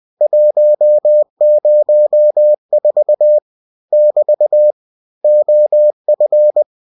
Morse code, heard as a keyed tone, about 20 wpm, 600 Hz. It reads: 104 <BT> OF
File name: cw-books / winnie-the-pooh / winnie-20wpm-page-0104